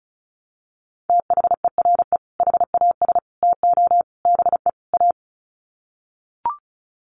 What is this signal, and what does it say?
Morse code, heard as a keyed tone, about 34 wpm, 700 Hz. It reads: THERE HAS TO BE A
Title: Sets of 5 Words Encore 35wpm